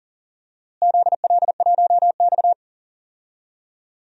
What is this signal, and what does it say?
Morse code, heard as a keyed tone, about 40 wpm, 700 Hz. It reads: ZL1X
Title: DXCC-100 Call Signs - 4 Characters - Repeated 3x - 40wpm